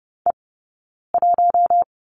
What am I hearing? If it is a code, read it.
Morse code, heard as a keyed tone, about 30 wpm, 700 Hz. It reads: E 1
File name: Lesson 20d - Number 3 - Two Characters 30wpm